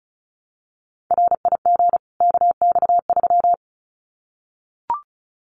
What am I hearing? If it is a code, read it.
Morse code, heard as a keyed tone, about 34 wpm, 700 Hz. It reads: RIG KX3
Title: Common QSO Phrases 35wpm